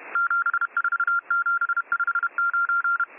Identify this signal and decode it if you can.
Morse code, heard as a keyed tone, about 31 wpm, 1390 Hz. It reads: 74759